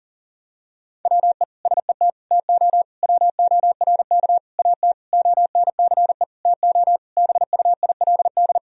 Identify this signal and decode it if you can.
Morse code, heard as a keyed tone, about 37 wpm, 700 Hz. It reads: WE SET TO WORK AT ONCE TO BUILD
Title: Sentences from Top 1000 Words - Review Part 1 - 40wpm